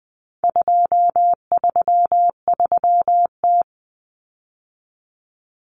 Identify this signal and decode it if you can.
Morse code, heard as a keyed tone, about 20 wpm, 700 Hz. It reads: 233T
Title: Common Cut Numbers - 4-Digit 20wpm